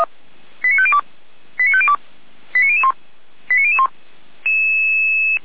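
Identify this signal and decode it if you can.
Five-tone selective call (ZVEI), heard as quick runs of five tones, tones of about 70 ms. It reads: 89502 89502 89001 89001
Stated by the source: ZVEI five-tone selective call received on SDR, recorded by feuerwehr-orlamuende fire brigade.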